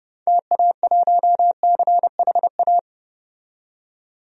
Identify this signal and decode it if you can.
Morse code, heard as a keyed tone, about 30 wpm, 700 Hz. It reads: TA1CHA
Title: DXCC-100 Call Signs - 6 Characters - 30wpm